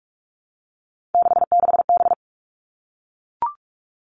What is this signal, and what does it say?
Morse code, heard as a keyed tone, about 44 wpm, 700 Hz. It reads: 66B
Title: All Cut Numbers - 3-Digit 45wpm